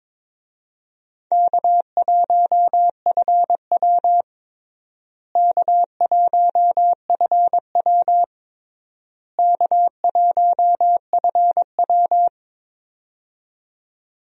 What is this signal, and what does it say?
Morse code, heard as a keyed tone, about 22 wpm, 700 Hz. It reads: K1FW K1FW K1FW
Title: USA Call Signs - 4 Characters - Repeated 3x - 22wpm